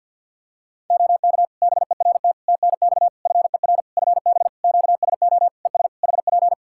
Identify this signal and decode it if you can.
Morse code, heard as a keyed tone, about 50 wpm, 700 Hz. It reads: OK BERT TNX FER FB QSO ES HP
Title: Practice QSO - Exchange 4 - Ending - 50wpm